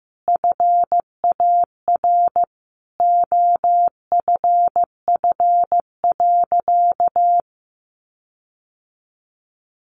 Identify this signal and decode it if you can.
Morse code, heard as a keyed tone, about 15 wpm, 700 Hz. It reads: FAR OFF.
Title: Sentences from Top 1400 Words 15wpm